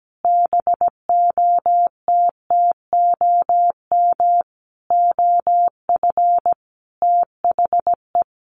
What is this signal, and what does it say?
Morse code, heard as a keyed tone, about 17 wpm, 700 Hz. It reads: BOTTOM OF THE
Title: Sets of 4 Words 17wpm